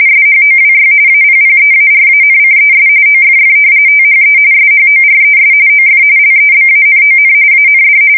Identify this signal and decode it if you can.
RTTY, about 45.45 baud. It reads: SHOWY APPEARANCES OFF THE SHORES OF LEBANON AND
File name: fsk170 / usnavy5